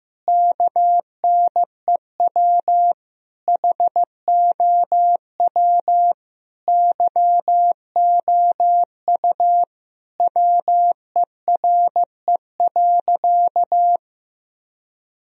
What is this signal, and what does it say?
Morse code, heard as a keyed tone, about 15 wpm, 700 Hz. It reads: KNEW HOW YOU WERE.
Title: Sentences from Top 1000 Words - Review Part 1 - 15wpm